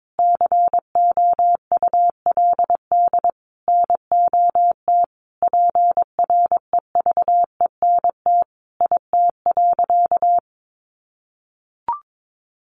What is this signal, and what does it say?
Morse code, heard as a keyed tone, about 22 wpm, 700 Hz. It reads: COULD NOT PREVENT IT.
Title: Sentences from Top 1100 Words 22wpm